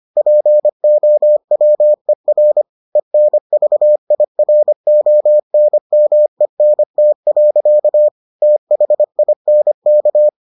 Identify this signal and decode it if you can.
Morse code, heard as a keyed tone, about 25 wpm, 600 Hz. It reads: POWER ENVIRONMENT. THINK